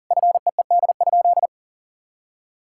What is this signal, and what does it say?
Morse code, heard as a keyed tone, about 40 wpm, 700 Hz. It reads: FEED?
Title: Sentences from Top 1600 Words 40wpm